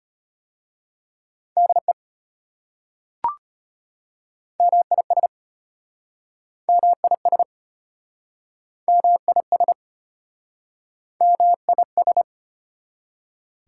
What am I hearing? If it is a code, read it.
Morse code, heard as a keyed tone, about 34 wpm, 700 Hz. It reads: DE MIS MIS MIS MIS